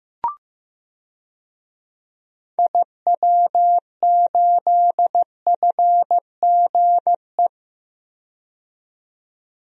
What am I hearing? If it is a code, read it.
Morse code, heard as a keyed tone, about 15 wpm, 700 Hz. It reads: IW8FGE
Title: DXCC-100 Call Signs - 6 Characters - 15wpm